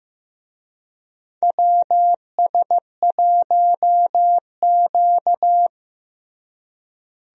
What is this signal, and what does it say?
Morse code, heard as a keyed tone, about 15 wpm, 700 Hz. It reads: WS1Q
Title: USA Call Signs - 4 Characters - Repeated 3x - 15wpm